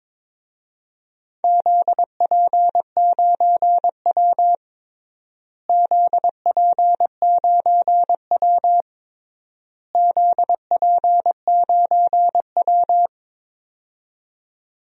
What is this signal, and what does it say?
Morse code, heard as a keyed tone, about 22 wpm, 700 Hz. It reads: ZP9W ZP9W ZP9W
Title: DXCC-100 Call Signs - 4 Characters - Repeated 3x - 22wpm